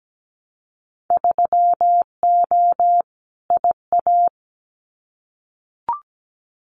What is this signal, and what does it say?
Morse code, heard as a keyed tone, about 17 wpm, 700 Hz. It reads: 3O IA
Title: Winter Field Day Exchanges 17wpm